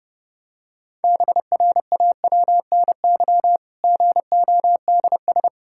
Morse code, heard as a keyed tone, about 30 wpm, 700 Hz. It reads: BRAWNY GODS
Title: Pangram Challenge v2 - 30wpm